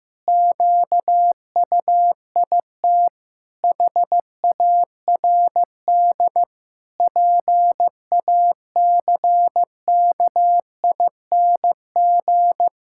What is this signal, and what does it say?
Morse code, heard as a keyed tone, about 15 wpm, 700 Hz. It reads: QUIT HARD PACKING
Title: Pangram Challenge v2 - 15wpm